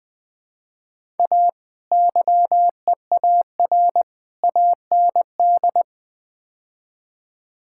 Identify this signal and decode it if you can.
Morse code, heard as a keyed tone, about 20 wpm, 700 Hz. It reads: A YEAR AND
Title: Build A Phrase - 5-Word Phrases - 20wpm